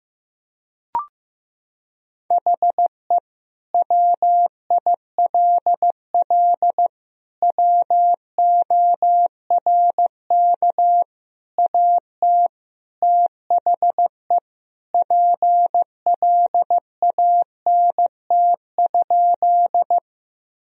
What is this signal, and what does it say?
Morse code, heard as a keyed tone, about 16 wpm, 700 Hz. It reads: HE WILL WORK AT THE PLANT?
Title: Sentences from Top 1000 Words - Review Part 1 - 15wpm